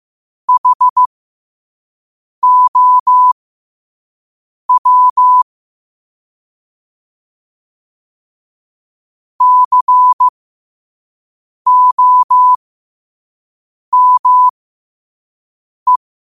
Morse code, heard as a keyed tone, about 15 wpm, spaced as for 5 wpm, 1000 Hz. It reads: HOW COME
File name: qotdfiles / 5wpm